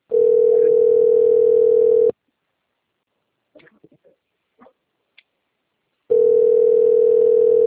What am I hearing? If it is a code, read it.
Ringback tone (North America), 440 + 480 Hz.